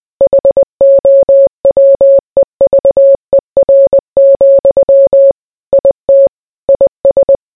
Morse code, heard as a keyed tone, about 20 wpm, 550 Hz. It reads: HOWEVER, IT IS